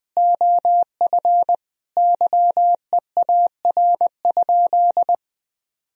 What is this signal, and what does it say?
Morse code, heard as a keyed tone, about 20 wpm, 700 Hz. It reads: OF YEAR?
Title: Sentences from Top 1000 Words - Review Part 3 - 20wpm